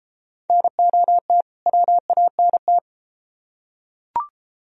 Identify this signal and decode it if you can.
Morse code, heard as a keyed tone, about 29 wpm, 700 Hz. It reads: NOT WANT